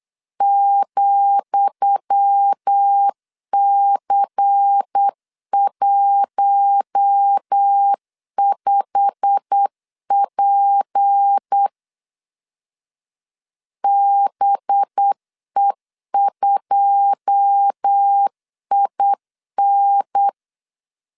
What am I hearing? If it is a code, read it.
Morse code, heard as a keyed tone, about 8 wpm, 800 Hz. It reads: ,C15P BE2IN
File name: LEZIONI / 10 SIGNS / 10D COMA 8wpm 3749